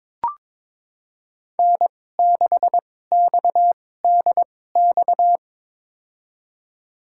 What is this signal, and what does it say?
Morse code, heard as a keyed tone, about 22 wpm, spaced as for 16 wpm, 700 Hz. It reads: N6XDX
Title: USA Call Signs - 2x Character Spacing - 22wpm